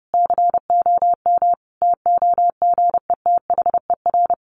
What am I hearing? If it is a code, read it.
Morse code, heard as a keyed tone, about 30 wpm, 700 Hz. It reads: COM TOGETHER
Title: Sentences from Top 1300 Words 30wpm